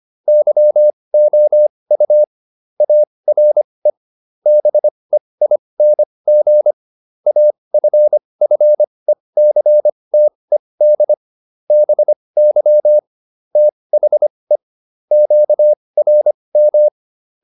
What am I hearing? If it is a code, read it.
Morse code, heard as a keyed tone, about 25 wpm, spaced as for 20 wpm, 600 Hz. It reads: YOU ARE BEING AFFECTED BY THE QRM
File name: SS209 20